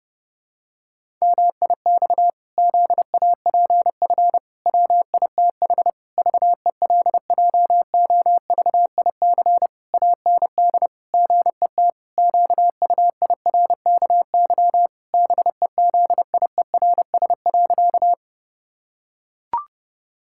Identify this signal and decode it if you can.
Morse code, heard as a keyed tone, about 30 wpm, 700 Hz. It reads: MIX ZAPF WITH VELJOVIC AND GET QUIRKY BEZIERS.
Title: Pangram Challenge v2 - 30wpm